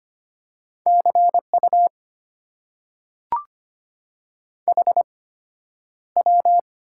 Morse code, heard as a keyed tone, about 25 wpm, spaced as for 11 wpm, 700 Hz. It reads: CU HW
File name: Top 90 QSO Elements - 8x Character Spacing - 25wpm